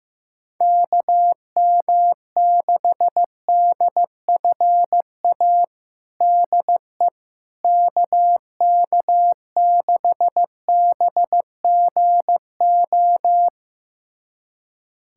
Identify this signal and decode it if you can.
Morse code, heard as a keyed tone, about 15 wpm, 700 Hz. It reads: KM6DFA DE KK6BGO
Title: USA Callsign DE Callsign - 6 Characters - 15wpm